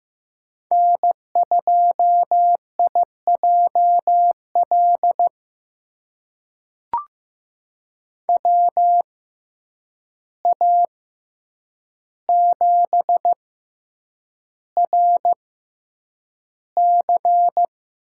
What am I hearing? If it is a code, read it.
Morse code, heard as a keyed tone, about 15 wpm, spaced as for 8 wpm, 700 Hz. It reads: N2IJL WA7RC
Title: USA Call Signs - 5 Characters - 6x Character Spacing - 15wpm